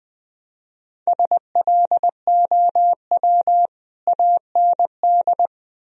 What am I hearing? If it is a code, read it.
Morse code, heard as a keyed tone, about 20 wpm, 700 Hz. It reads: SLOW AND
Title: Sentences from Top 1000 Words - Review Part 2 - 20wpm